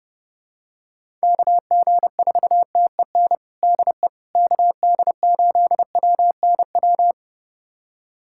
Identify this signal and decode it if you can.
Morse code, heard as a keyed tone, about 30 wpm, 700 Hz. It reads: KG4TEN DE KD8WNW
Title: USA Callsign DE Callsign - 6 Characters - 30wpm